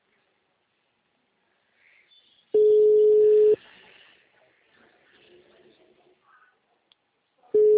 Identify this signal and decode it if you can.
Ringback tone (Europe, 425 Hz).
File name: out- -2041- - - .16627